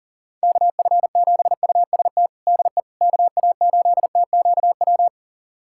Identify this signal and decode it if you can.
Morse code, heard as a keyed tone, about 40 wpm, 700 Hz. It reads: KF7UST DE KA8TQW
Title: USA Callsign DE Callsign 40wpm